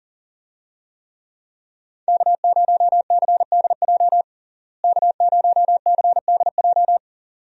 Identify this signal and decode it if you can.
Morse code, heard as a keyed tone, about 37 wpm, 700 Hz. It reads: K0CDJ K0CDJ
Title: USA Call Signs - 5 Characters - Repeated 2x - 40wpm